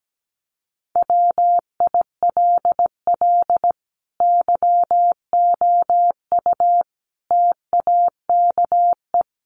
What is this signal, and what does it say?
Morse code, heard as a keyed tone, about 17 wpm, 700 Hz. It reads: WILL YOU TAKE
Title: Sentences from Top 900 Words 17wpm